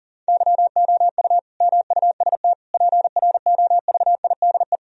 Morse code, heard as a keyed tone, about 40 wpm, 700 Hz. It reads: YOU MUST PROVIDE
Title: Sentences from Top 1000 Words - Review Part 3 - 40wpm